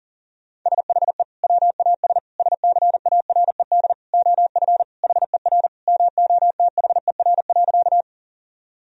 Morse code, heard as a keyed tone, about 40 wpm, 700 Hz. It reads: SHE WAS SCARED OF HER MOTHER.
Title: Sentences from Top 1000 Words - Review Part 4 - 40wpm